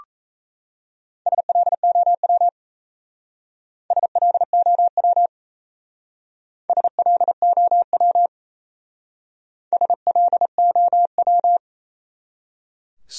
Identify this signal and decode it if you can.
Morse code, heard as a keyed tone, about 35 wpm, 700 Hz. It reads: SLOW SLOW SLOW SLOW
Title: Speed-Racing Top 500 Words 28wpm